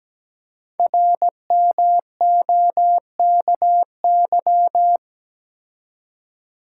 Morse code, heard as a keyed tone, about 17 wpm, 700 Hz. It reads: RMOKY
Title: Code Groups - 5 Letters - 17wpm